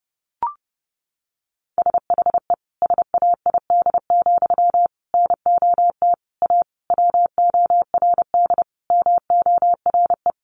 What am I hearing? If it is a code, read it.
Morse code, heard as a keyed tone, about 30 wpm, 700 Hz. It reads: SHE SAID, NOT A WORD MORE
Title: Sentences from Top 800 Words 30wpm